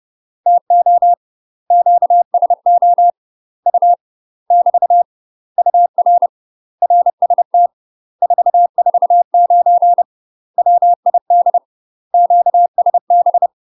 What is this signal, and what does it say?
Morse code, heard as a keyed tone, about 30 wpm, 700 Hz. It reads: TO QSO U <BT> UR RST 449 WID QSB
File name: UK MASTER QSOs TRAINING / 30WPM / UK MASTER QSOs 0050